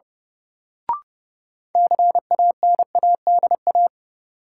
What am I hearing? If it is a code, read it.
Morse code, heard as a keyed tone, about 30 wpm, 700 Hz. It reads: CANADA
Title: Sets of 5 Words - New Word Drill - 30wpm